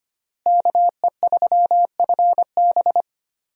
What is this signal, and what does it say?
Morse code, heard as a keyed tone, about 25 wpm, 700 Hz. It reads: KE3FB
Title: USA Call Signs - 5 Characters - Repeated 2x - 25wpm